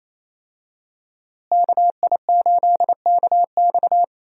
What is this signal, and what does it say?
Morse code, heard as a keyed tone, about 28 wpm, 700 Hz. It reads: KI8KX
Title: USA Call Signs - 5 Characters - Repeated 3x - 28wpm